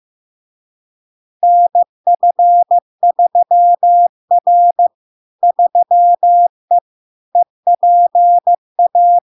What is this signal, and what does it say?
Morse code, heard as a keyed tone, about 15 wpm, 700 Hz. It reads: NF3R 3E EPA
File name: LARK-FD-2022-Log-in-CW-15wpm-700-Hz